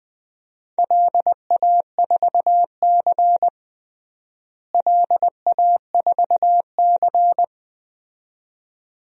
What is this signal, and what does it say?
Morse code, heard as a keyed tone, about 20 wpm, 700 Hz. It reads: LA4C LA4C
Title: DXCC-100 Call Signs - 4 Characters - Repeated 2x - 20wpm